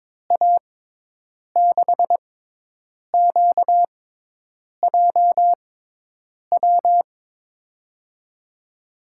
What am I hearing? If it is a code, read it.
Morse code, heard as a keyed tone, about 22 wpm, spaced as for 8 wpm, 700 Hz. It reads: A6QJW
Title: USA Call Signs - 6 Characters - 6x Character Spacing - 22wpm